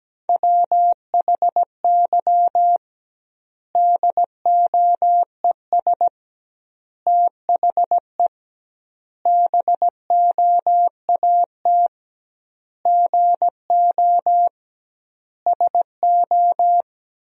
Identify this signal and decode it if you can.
Morse code, heard as a keyed tone, about 17 wpm, 700 Hz. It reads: WHY DOES THE BOAT GO SO
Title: 2x Word-Spacing - Sentences from Top 500 Words - 17wpm